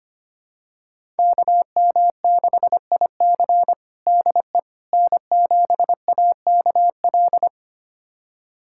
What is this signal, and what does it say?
Morse code, heard as a keyed tone, about 25 wpm, 700 Hz. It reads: KM6IC DE N7AKL
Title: USA Callsign DE Callsign - 5 Characters - 25wpm